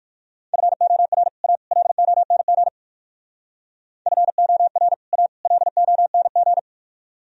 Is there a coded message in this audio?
Yes: Morse code from 0.5 to 6.6 s, heard as a keyed tone, about 58 wpm, 700 Hz. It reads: FOR A LONG FOR A LONG